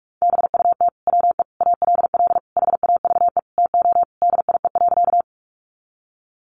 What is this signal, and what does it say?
Morse code, heard as a keyed tone, about 44 wpm, 700 Hz. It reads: BUT WE ALL HAVE TO DIE.
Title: Sentences from Top 1300 Words 45wpm